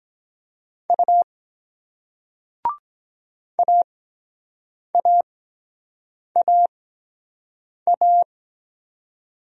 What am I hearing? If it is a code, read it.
Morse code, heard as a keyed tone, about 23 wpm, spaced as for 7 wpm, 700 Hz. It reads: U AAAA